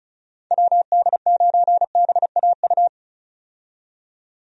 Morse code, heard as a keyed tone, about 36 wpm, 700 Hz. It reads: WD9BAU
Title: USA Call Signs - 6 Characters - 35wpm